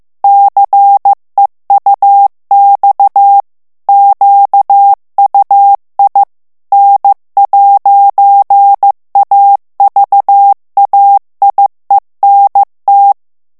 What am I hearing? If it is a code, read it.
Morse code, heard as a keyed tone, about 15 wpm, 800 Hz. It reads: CEUX QUI N'AVAIENT
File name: morse mp3 / FRANCAIS TEXTE CLAIR 2 VITESSE 14 MOTS MINUTES